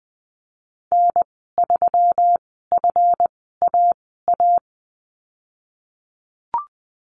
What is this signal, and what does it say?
Morse code, heard as a keyed tone, about 19 wpm, spaced as for 14 wpm, 700 Hz. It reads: N3FAA
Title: 140 Lesson 23g - Letter f - Call Signs 20wpm